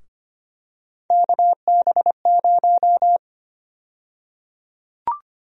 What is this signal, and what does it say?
Morse code, heard as a keyed tone, about 25 wpm, 700 Hz. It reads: KB0
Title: USA Call Sign Prefixes - Repeated 2x - 25wpm